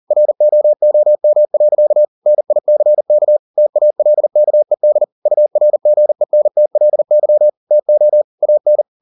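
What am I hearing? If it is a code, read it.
Morse code, heard as a keyed tone, about 40 wpm, 600 Hz. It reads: ROOM. NICK TALKED URGENTLY TO AN
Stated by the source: Morse code at 40 wpm, practice text, CW med Gnister.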